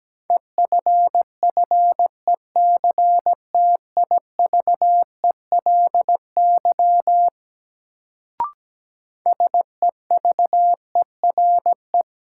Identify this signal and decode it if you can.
Morse code, heard as a keyed tone, about 17 wpm, 700 Hz. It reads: EFFECTIVELY SEVERE
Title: Sentences from Top 1400 Words - New Word Drill - 17wpm